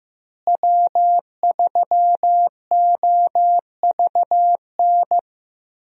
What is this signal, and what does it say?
Morse code, heard as a keyed tone, about 16 wpm, 700 Hz. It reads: W3OVN
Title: USA Call Signs - 5 Characters - Repeated 3x - 15wpm